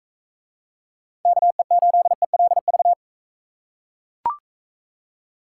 Morse code, heard as a keyed tone, about 40 wpm, 700 Hz. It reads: KE8ELV